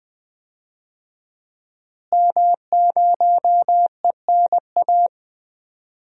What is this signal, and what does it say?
Morse code, heard as a keyed tone, about 21 wpm, 700 Hz. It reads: M0ENA
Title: DXCC-100 Call Signs - 5 Characters - Repeated 3x - 20wpm